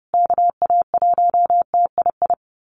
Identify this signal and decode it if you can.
Morse code, heard as a keyed tone, about 30 wpm, 700 Hz. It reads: KA1TII
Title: USA Call Signs Repeated 3x 30wpm